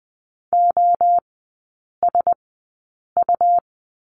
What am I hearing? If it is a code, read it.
Morse code, heard as a keyed tone, about 20 wpm, spaced as for 8 wpm, 700 Hz. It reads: OSU
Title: 178 Lesson 29e - Slash - Three Characters 20wpm